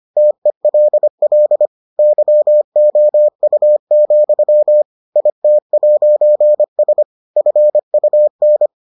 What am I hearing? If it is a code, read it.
Morse code, heard as a keyed tone, about 25 wpm, 600 Hz. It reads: TELL YOU, IT'S FUN